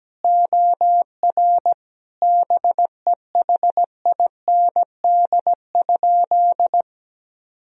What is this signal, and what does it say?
Morse code, heard as a keyed tone, about 17 wpm, 700 Hz. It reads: OR BEHIND?
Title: Sentences from Top 1000 Words - Review Part 2 - 17wpm